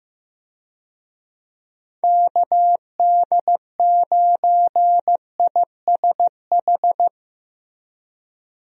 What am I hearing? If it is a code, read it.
Morse code, heard as a keyed tone, about 15 wpm, 700 Hz. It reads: KD9ISH
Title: USA Call Signs - 6 Characters - Repeated 3x - 15wpm